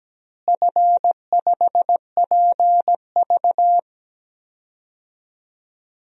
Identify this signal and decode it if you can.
Morse code, heard as a keyed tone, about 17 wpm, 700 Hz. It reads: F5PV
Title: DXCC-100 Call Signs - 4 Characters - Repeated 2x - 17wpm